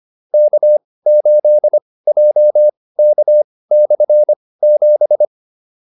Morse code, heard as a keyed tone, about 25 wpm, spaced as for 18 wpm, 600 Hz. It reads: K8JK/7